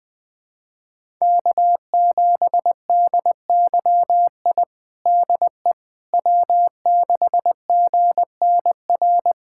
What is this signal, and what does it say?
Morse code, heard as a keyed tone, about 20 wpm, 700 Hz. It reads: K7DYI DE W6GNR
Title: USA Callsign DE Callsign - 5 Characters - 20wpm